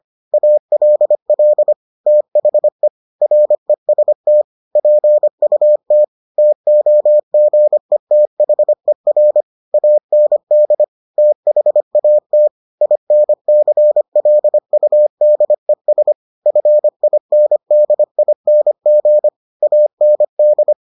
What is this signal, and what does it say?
Morse code, heard as a keyed tone, about 25 wpm, 600 Hz. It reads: ALL THE REST PUT TOGETHER AND THAT INCLUDES FINDING AND